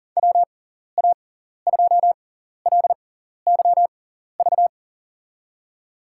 Morse code, heard as a keyed tone, about 40 wpm, spaced as for 14 wpm, 700 Hz. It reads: WA2LYV
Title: USA Call Signs - 6 Characters - 6x Character Spacing - 40wpm